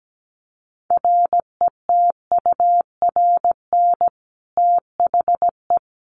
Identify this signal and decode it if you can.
Morse code, heard as a keyed tone, about 17 wpm, 700 Hz. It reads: RETURN THE
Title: Sentences from Top 1200 Words 17wpm